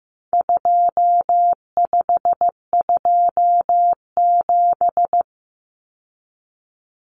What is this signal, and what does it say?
Morse code, heard as a keyed tone, about 15 wpm, 700 Hz. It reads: 2527
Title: Numbers - 4-Digit 15wpm